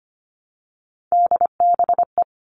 Morse code, heard as a keyed tone, about 25 wpm, 700 Hz. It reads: DBE
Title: All Cut Numbers - 3-Digit 25wpm